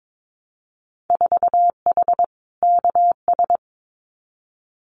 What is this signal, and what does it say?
Morse code, heard as a keyed tone, about 22 wpm, 700 Hz. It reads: 4H KS
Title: Winter Field Day Exchanges Encore 22wpm